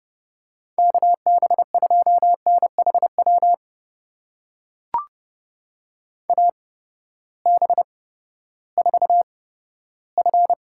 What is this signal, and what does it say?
Morse code, heard as a keyed tone, about 31 wpm, spaced as for 15 wpm, 700 Hz. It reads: KB2NHW AB4F
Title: USA Call Signs - 8x Character Spacing - 30wpm